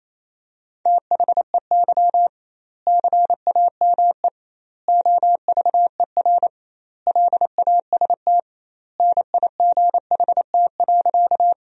Morse code, heard as a keyed tone, about 27 wpm, 700 Hz. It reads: THEY CAME OVER LAST NIGHT.
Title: 2x Word-Spacing - Sentences from Top 300 Words - 28wpm